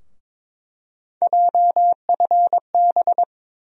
Morse code, heard as a keyed tone, about 22 wpm, 700 Hz. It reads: JFB
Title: USA Call Sign Postfixes - 3 Letters 22wpm